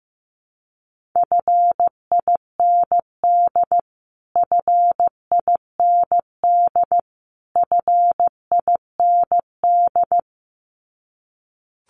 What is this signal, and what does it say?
Morse code, heard as a keyed tone, about 15 wpm, 700 Hz. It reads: FIND FIND FIND
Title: Top 500 Words Repeated 15wpm